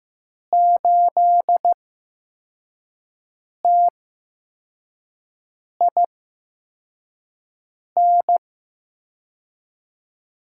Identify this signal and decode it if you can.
Morse code, heard as a keyed tone, about 15 wpm, spaced as for 4 wpm, 700 Hz. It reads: 8TIN
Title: USA Call Signs - 6 Characters - 8x Character Spacing - 15wpm